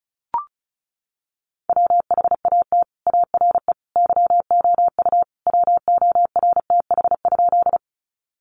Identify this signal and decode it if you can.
Morse code, heard as a keyed tone, about 34 wpm, 700 Hz. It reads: WHAT ARE YOU WORTH?
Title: Sentences from Top 800 Words 35wpm